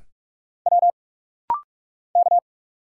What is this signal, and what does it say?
Morse code, heard as a keyed tone, about 46 wpm, 700 Hz. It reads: W K
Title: Single Letters - Rapid-Fire 45wpm